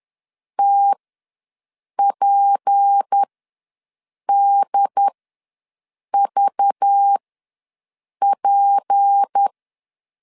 Morse code, heard as a keyed tone, about 10 wpm, spaced as for 6 wpm, 800 Hz. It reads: TPDVP
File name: LEZIONI / 6 PKX0 / 6 P ws 10wpm 3751